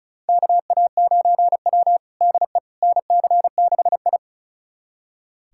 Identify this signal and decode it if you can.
Morse code, heard as a keyed tone, about 36 wpm, 700 Hz. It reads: KA9W DE NC6I
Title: USA Callsign DE Callsign - 4 Characters - 35wpm